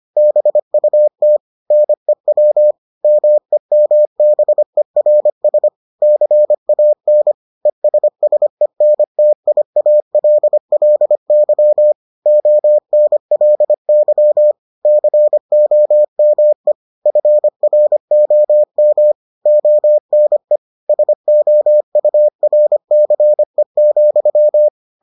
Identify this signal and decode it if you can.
Morse code, heard as a keyed tone, about 25 wpm, 600 Hz. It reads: BUT NEW MEMBERS CAN ESSENTIALLY ONLY COME FROM ONE SOURCE,